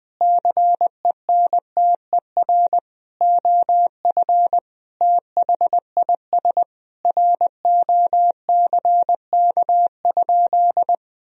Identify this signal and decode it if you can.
Morse code, heard as a keyed tone, about 20 wpm, 700 Hz. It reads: CENTER OF THIS ROCK?
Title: Sentences from Top 1000 Words - Review Part 2 - 20wpm